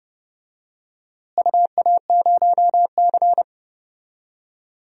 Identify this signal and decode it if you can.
Morse code, heard as a keyed tone, about 31 wpm, 700 Hz. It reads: UA0C
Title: DXCC-100 Call Signs - 4 Characters - Repeated 3x - 30wpm